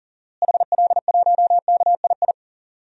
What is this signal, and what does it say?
Morse code, heard as a keyed tone, about 40 wpm, 700 Hz. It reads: HL1KII
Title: DXCC-100 Call Signs - 6 Characters - Repeated 2x - 40wpm